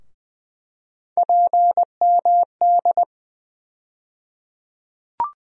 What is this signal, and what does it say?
Morse code, heard as a keyed tone, about 20 wpm, 700 Hz. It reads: PMD
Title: USA Call Sign Postfixes - 3 Letters 20wpm